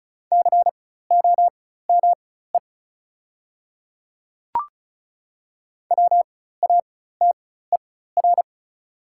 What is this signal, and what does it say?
Morse code, heard as a keyed tone, about 34 wpm, spaced as for 16 wpm, 700 Hz. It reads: COME WATER
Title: 4x Character Spacing - Top 200 Words - 35wpm